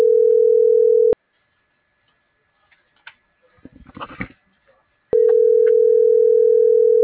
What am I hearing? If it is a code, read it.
Ringback tone (North America), 440 + 480 Hz.